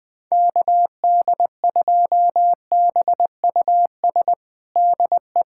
Morse code, heard as a keyed tone, about 20 wpm, 700 Hz. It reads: KD2BUS DE
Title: USA Callsign DE Callsign 20wpm